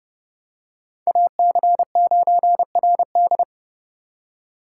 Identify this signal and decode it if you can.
Morse code, heard as a keyed tone, about 29 wpm, 700 Hz. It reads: AC9RD